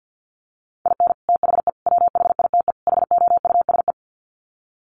Morse code, heard as a keyed tone, about 50 wpm, 700 Hz. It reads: IN THE WHITE HOUSE
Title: Sets of 4 Words 50wpm